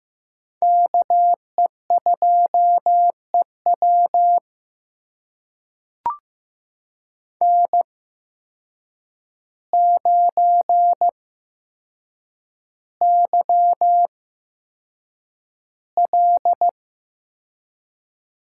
Morse code, heard as a keyed tone, about 15 wpm, spaced as for 7 wpm, 700 Hz. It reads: KE2EW N9YL
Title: USA Call Signs - 5 Characters - 8x Character Spacing - 15wpm